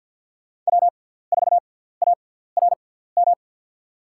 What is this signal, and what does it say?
Morse code, heard as a keyed tone, about 53 wpm, spaced as for 17 wpm, 700 Hz. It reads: W4ARM
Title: USA Call Signs - 5 Characters - 6x Character Spacing - 50wpm